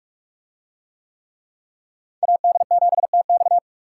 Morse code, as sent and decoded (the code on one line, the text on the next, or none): .- -.. --... - -..-
AD7TX